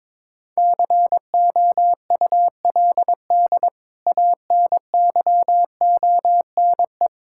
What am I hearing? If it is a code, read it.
Morse code, heard as a keyed tone, about 22 wpm, 700 Hz. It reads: COULD ANYONE
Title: Sentences from Top 1600 Words 22wpm